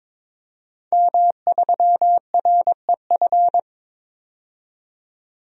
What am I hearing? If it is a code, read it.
Morse code, heard as a keyed tone, about 21 wpm, 700 Hz. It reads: M3REF
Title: DXCC-100 Call Signs - 5 Characters - 22wpm